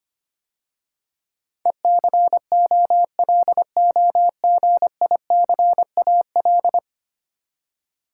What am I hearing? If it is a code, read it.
Morse code, heard as a keyed tone, about 25 wpm, 700 Hz. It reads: ECOLOGICAL
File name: 10 Letter Words - First 3 Letters Given - 25wpm